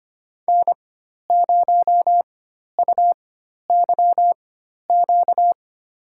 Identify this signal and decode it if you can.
Morse code, heard as a keyed tone, about 25 wpm, spaced as for 12 wpm, 700 Hz. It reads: N0UYQ
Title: USA Call Signs - 4x Character Spacing - 25wpm